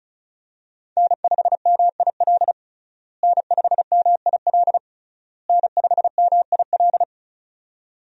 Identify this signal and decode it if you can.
Morse code, heard as a keyed tone, about 36 wpm, 700 Hz. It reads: N5MIL N5MIL N5MIL
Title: USA Call Signs - 5 Characters - Repeated 3x - 35wpm